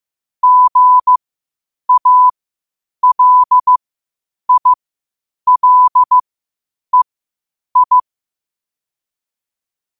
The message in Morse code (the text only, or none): GALILEI